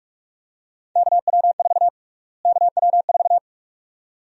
Morse code, heard as a keyed tone, about 44 wpm, 690 Hz. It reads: KW4 KW4
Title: USA Call Sign Prefixes - Repeated 2x - 45wpm